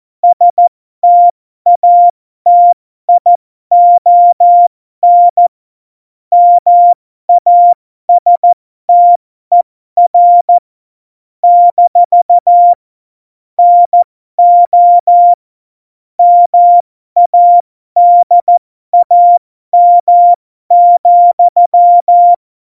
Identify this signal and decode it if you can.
Morse code, heard as a keyed tone, about 14 wpm, spaced as for 12 wpm, 700 Hz. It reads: STATION MASTER - NO MADAM,
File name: Morsekurs / Witze